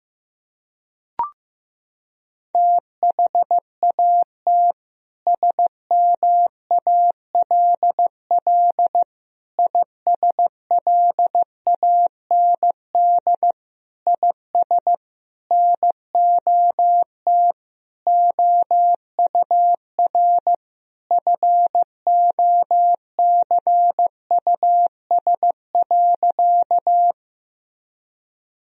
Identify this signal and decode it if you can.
Morse code, heard as a keyed tone, about 15 wpm, 700 Hz. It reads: THAT SMALL ISLAND IS NOT OUR FOCUS.
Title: Sentences from Top 1000 Words - Review Part 4 - 15wpm